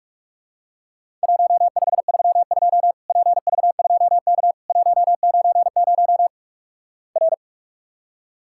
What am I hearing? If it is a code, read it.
Morse code, heard as a keyed tone, about 47 wpm, 700 Hz. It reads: 1532 PV2K 190 R
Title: Worked All Europe DX Contest - QTC Exchange - 45wpm